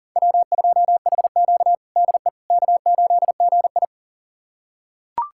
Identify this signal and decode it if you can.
Morse code, heard as a keyed tone, about 37 wpm, 700 Hz. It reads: W2HQ DE K8GI
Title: USA Callsign DE Callsign - 4 Characters - 40wpm